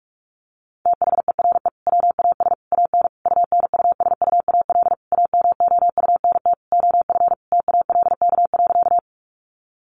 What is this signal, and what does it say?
Morse code, heard as a keyed tone, about 44 wpm, 700 Hz. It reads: THERE WAS AN UNUSUAL AMOUNT OF TALK.
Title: Sentences from Top 1400 Words 45wpm